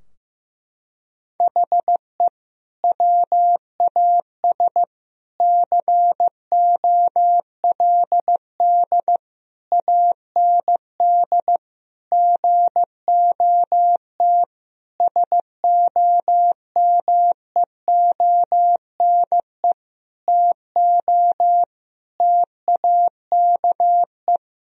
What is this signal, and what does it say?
Morse code, heard as a keyed tone, about 15 wpm, 700 Hz. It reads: HE WAS COLD AND GOT SOMEONE TO TAKE
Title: Sentences from Top 1000 Words - Review Part 4 - 15wpm